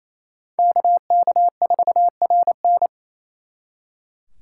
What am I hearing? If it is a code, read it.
Morse code, heard as a keyed tone, about 37 wpm, 700 Hz. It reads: KK4RN